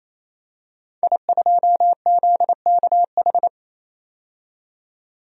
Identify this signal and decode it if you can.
Morse code, heard as a keyed tone, about 28 wpm, 700 Hz. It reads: I2ZKH
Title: DXCC-100 Call Signs - 5 Characters - 28wpm